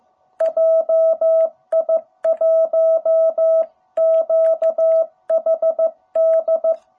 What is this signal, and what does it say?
Morse code, heard as a keyed tone, about 15 wpm, 660 Hz. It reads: JI1QHD
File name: jh8cbh / 2021cw